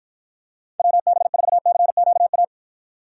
Morse code, heard as a keyed tone, about 54 wpm, 700 Hz. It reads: WB4CQA